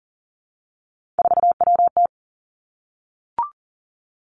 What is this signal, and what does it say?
Morse code, heard as a keyed tone, about 37 wpm, 700 Hz. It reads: 4WT
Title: Three Challenging Letter-Number Combinations 40wpm